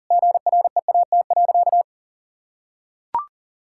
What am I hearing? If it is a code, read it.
Morse code, heard as a keyed tone, about 37 wpm, 700 Hz. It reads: GREAT.
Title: Sentences from Top 1600 Words 40wpm